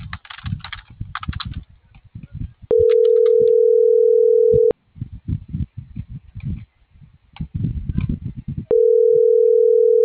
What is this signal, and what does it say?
Ringback tone (North America), 440 + 480 Hz.